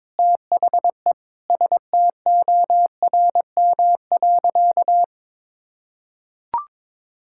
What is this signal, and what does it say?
Morse code, heard as a keyed tone, about 22 wpm, 700 Hz. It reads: THE STORM.
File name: practice / 22wpm / Sentences with 4-Word Phrases - 22wpm